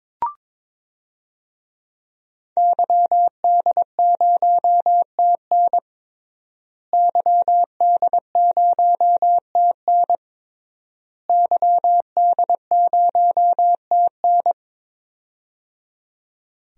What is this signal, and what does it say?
Morse code, heard as a keyed tone, about 22 wpm, 700 Hz. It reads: YD0TN YD0TN YD0TN
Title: DXCC-100 Call Signs - 5 Characters - Repeated 3x - 22wpm